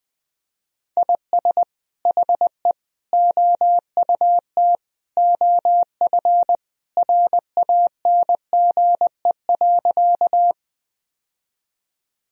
Morse code, heard as a keyed tone, about 20 wpm, 700 Hz. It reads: IS HE OUT OF RANGE.
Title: Sentences from Top 1000 Words - Review Part 4 - 20wpm